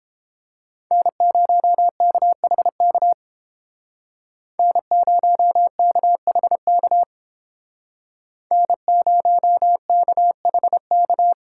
Morse code, heard as a keyed tone, about 30 wpm, 700 Hz. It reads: N0KHK N0KHK N0KHK